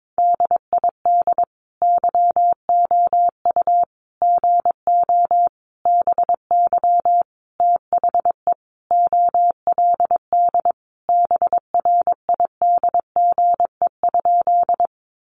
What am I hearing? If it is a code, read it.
Morse code, heard as a keyed tone, about 22 wpm, 700 Hz. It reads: DID YOU GO BY THE OLD BRIDGE?
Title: Sentences from Top 1200 Words 22wpm